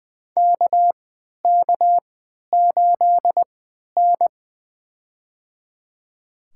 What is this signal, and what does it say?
Morse code, heard as a keyed tone, about 20 wpm, spaced as for 11 wpm, 700 Hz. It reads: KK8N
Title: USA Call Signs - 4 Characters - 3x Character Spacing - 20wpm